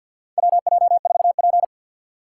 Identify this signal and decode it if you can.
Morse code, heard as a keyed tone, about 53 wpm, 700 Hz. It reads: WJ4P
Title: USA Call Signs - 4 Characters - 3x Character Spacing - 50wpm